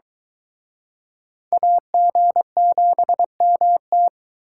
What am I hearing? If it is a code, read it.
Morse code, heard as a keyed tone, about 23 wpm, 700 Hz. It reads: AG7MT